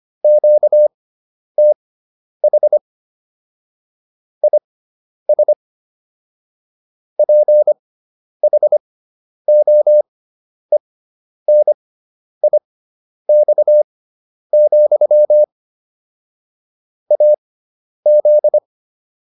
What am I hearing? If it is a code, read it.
Morse code, heard as a keyed tone, about 25 wpm, spaced as for 10 wpm, 600 Hz. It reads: QTH IS PHOENIX, AZ
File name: QSO306 10